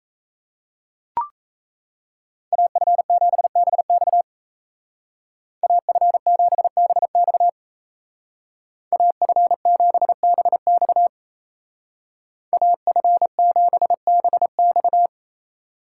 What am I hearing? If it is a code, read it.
Morse code, heard as a keyed tone, about 34 wpm, 700 Hz. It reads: AF7BX AF7BX AF7BX AF7BX
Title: Speed Racing - USA 2x2 Call Signs 28wpm